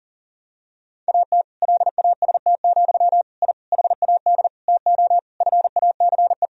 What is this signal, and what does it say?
Morse code, heard as a keyed tone, about 37 wpm, 700 Hz. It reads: AT LAST, I HAD TO FACE
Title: Sentences from Top 1000 Words - Review Part 2 - 40wpm